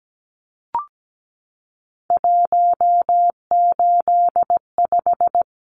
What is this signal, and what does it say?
Morse code, heard as a keyed tone, about 17 wpm, 700 Hz. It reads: 185
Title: Numbers - 3-Digit 17wpm